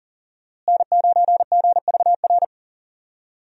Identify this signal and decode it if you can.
Morse code, heard as a keyed tone, about 37 wpm, 700 Hz. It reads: N9GVR
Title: USA Call Signs - 5 Characters - Repeated 2x - 40wpm